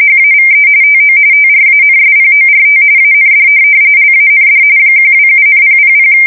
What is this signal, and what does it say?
RTTY, about 45.45 baud. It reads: HAVE TO TRANSPORT AND LAND 127,000
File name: fsk170 / USS-New-Jersey